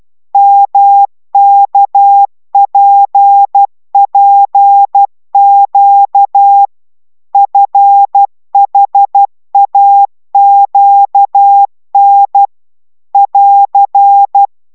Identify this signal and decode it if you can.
Morse code, heard as a keyed tone, about 12 wpm, 800 Hz. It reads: MKPPQ FHAQN <AR>